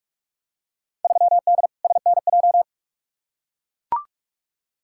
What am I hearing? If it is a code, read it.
Morse code, heard as a keyed tone, about 44 wpm, 700 Hz. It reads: 3D SNJ
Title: Field Day Exchanges Encore - 45wpm